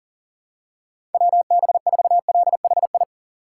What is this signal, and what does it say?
Morse code, heard as a keyed tone, about 40 wpm, 700 Hz. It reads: WB4LHI
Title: USA Call Signs - 6 Characters - 2x Character Spacing - 40wpm